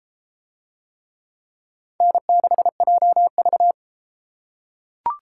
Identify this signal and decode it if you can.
Morse code, heard as a keyed tone, about 31 wpm, 700 Hz. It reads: N6JV